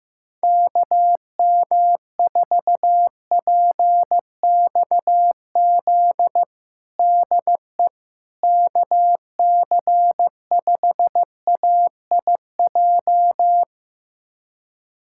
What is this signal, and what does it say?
Morse code, heard as a keyed tone, about 15 wpm, 700 Hz. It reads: KM4PXZ DE KC5AIJ
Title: USA Callsign DE Callsign - 6 Characters - 15wpm